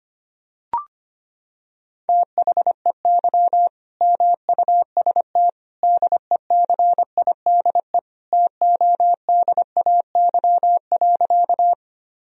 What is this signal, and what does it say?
Morse code, heard as a keyed tone, about 25 wpm, 700 Hz. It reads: THEY MUST DECIDE TODAY.
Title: Sentences from Top 1000 Words - Review Part 3 - 25wpm